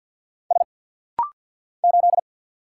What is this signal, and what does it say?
Morse code, heard as a keyed tone, about 50 wpm, 700 Hz. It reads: S 8
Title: Single Letter-Number - Rapid-Fire 50wpm